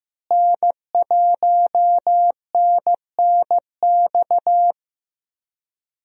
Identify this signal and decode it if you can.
Morse code, heard as a keyed tone, about 15 wpm, 700 Hz. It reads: N1NNX
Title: USA Call Signs - 5 Characters - Repeated 3x - 15wpm